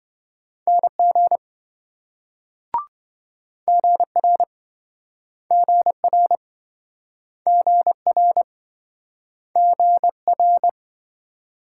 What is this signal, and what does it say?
Morse code, heard as a keyed tone, about 27 wpm, 700 Hz. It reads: NG GR GR GR GR